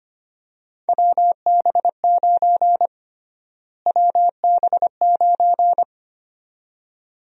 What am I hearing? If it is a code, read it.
Morse code, heard as a keyed tone, about 25 wpm, 700 Hz. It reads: WB9 WB9
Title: USA Call Sign Prefixes - Repeated 2x - 25wpm